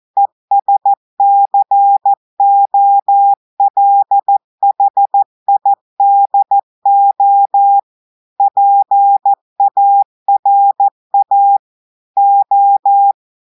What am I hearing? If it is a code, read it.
Morse code, heard as a keyed tone, about 14 wpm, 800 Hz. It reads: ESCOLHIDO PARA O
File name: audios / portugais / portugais texte en clair 1 vitesse 14 mots minutes